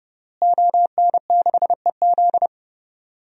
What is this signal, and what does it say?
Morse code, heard as a keyed tone, about 30 wpm, 700 Hz. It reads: ON6EZ
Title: DXCC-100 Call Signs - 5 Characters - Repeated 3x - 30wpm